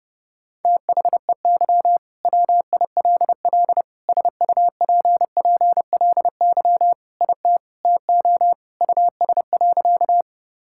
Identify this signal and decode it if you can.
Morse code, heard as a keyed tone, about 30 wpm, 700 Hz. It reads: THEY WILL SUPPLY IT TO US.
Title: Sentences from Top 1000 Words - Review Part 5 - 30wpm